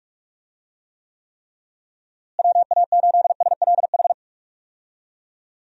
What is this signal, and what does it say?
Morse code, heard as a keyed tone, about 42 wpm, 700 Hz. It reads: WA8SLH
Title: USA Call Signs - 6 Characters - Repeated 3x - 45wpm